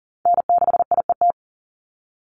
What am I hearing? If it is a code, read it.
Morse code, heard as a keyed tone, about 40 wpm, 700 Hz. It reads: N6IET
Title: Call Signs Repeated 2x - 40wpm